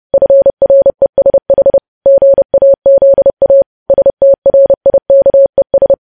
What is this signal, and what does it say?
Morse code, heard as a keyed tone, about 30 wpm, 550 Hz. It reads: FRESH GAZA STRIKES
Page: news headlines in morse code at 30 wpm